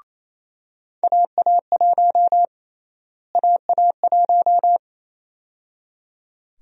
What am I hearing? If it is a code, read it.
Morse code, heard as a keyed tone, about 28 wpm, 700 Hz. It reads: AA1 AA1
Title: USA Call Sign Prefixes - Repeated 2x - 28wpm